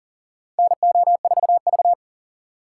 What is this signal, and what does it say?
Morse code, heard as a keyed tone, about 40 wpm, 700 Hz. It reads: NO4V
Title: USA Call Signs - 4 Characters - 3x Character Spacing - 40wpm